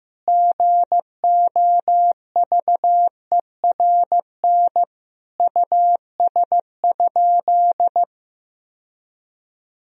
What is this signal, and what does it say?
Morse code, heard as a keyed tone, about 15 wpm, 700 Hz. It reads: GOVERN US?
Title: Sentences from Top 1000 Words - Review Part 2 - 15wpm